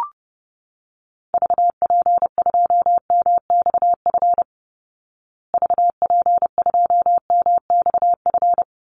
Morse code, heard as a keyed tone, about 28 wpm, 700 Hz. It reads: VP2MXF VP2MXF
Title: Call Signs Repeated 3x - 30wpm